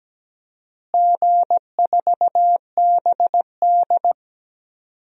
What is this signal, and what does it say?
Morse code, heard as a keyed tone, about 17 wpm, 700 Hz. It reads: G4BD
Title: DXCC-100 Call Signs - 4 Characters - Repeated 3x - 17wpm